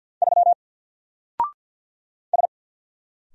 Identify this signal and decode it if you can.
Morse code, heard as a keyed tone, about 48 wpm, 700 Hz. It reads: 3 S
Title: Single Letter-Number Alt 50wpm